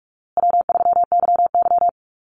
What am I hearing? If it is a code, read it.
Morse code, heard as a keyed tone, about 44 wpm, 700 Hz. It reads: W3YY
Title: Easy Call Signs Repeated 3x - 45wpm